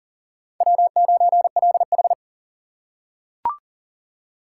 Morse code, heard as a keyed tone, about 40 wpm, 700 Hz. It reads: W9LH
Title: USA Call Signs - 4 Characters - 2x Character Spacing - 40wpm